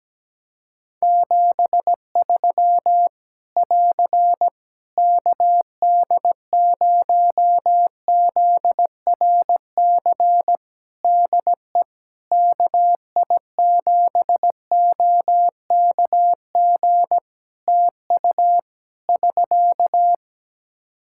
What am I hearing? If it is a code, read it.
Morse code, heard as a keyed tone, about 17 wpm, 700 Hz. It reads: 73 <AR> KD0ZRC DE KI7OKG TU <SK>
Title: K1USN SST QSO Practice 17wpm